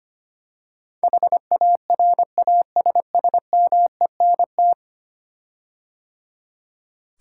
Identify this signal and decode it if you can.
Morse code, heard as a keyed tone, about 25 wpm, 700 Hz. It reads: HARASSMENT
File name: 10 Letter Words - First 3 Letters Given - 25wpm